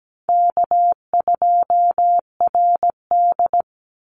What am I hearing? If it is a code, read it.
Morse code, heard as a keyed tone, about 17 wpm, 700 Hz. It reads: K2RD
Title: Call Signs - 17wpm